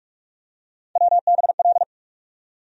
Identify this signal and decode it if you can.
Morse code, heard as a keyed tone, about 46 wpm, 700 Hz. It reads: WBL
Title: USA Call Sign Postfixes - 3 Letters 45wpm